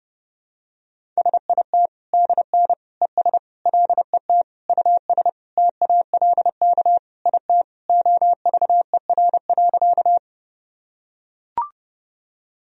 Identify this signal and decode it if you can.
Morse code, heard as a keyed tone, about 31 wpm, 700 Hz. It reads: SIT DN ES LET US TALK IT OVER.
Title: Sentences from Top 500 Words and Abbreviations 30wpm